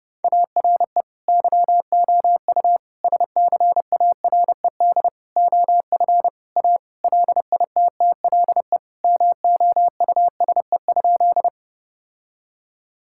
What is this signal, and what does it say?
Morse code, heard as a keyed tone, about 30 wpm, 700 Hz. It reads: ARE YOU SCARED OF A LITTLE MOUSE?
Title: Sentences from Top 1000 Words - Review Part 5 - 30wpm